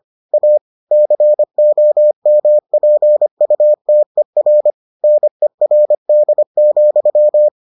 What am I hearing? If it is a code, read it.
Morse code, heard as a keyed tone, about 25 wpm, 600 Hz. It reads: A COMPUTER NERD,